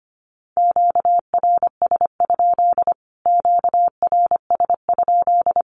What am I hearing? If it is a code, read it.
Morse code, heard as a keyed tone, about 25 wpm, 700 Hz. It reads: QRS? QRS?
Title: Q-Codes Repeated 3x 25wpm